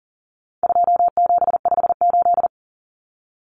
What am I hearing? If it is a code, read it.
Morse code, heard as a keyed tone, about 40 wpm, 700 Hz. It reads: 2758
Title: Numbers - 4-Digit 40wpm